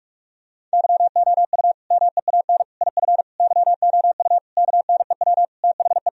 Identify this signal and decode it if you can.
Morse code, heard as a keyed tone, about 46 wpm, 700 Hz. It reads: YOU MEAN IF YOU KNEW THE
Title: Sentences from Top 1000 Words - Review Part 4 - 45wpm